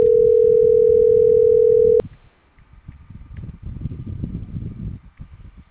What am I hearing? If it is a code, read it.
Ringback tone (North America), 440 + 480 Hz.